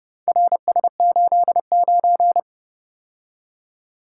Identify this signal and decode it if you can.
Morse code, heard as a keyed tone, about 31 wpm, 700 Hz. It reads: RS89